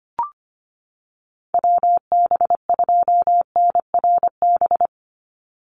Morse code, heard as a keyed tone, about 25 wpm, 700 Hz. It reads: WB2NRB
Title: USA Call Signs 25wpm